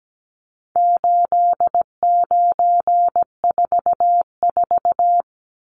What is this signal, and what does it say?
Morse code, heard as a keyed tone, about 17 wpm, 700 Hz. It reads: 8944
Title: Numbers - 4-Digit 17wpm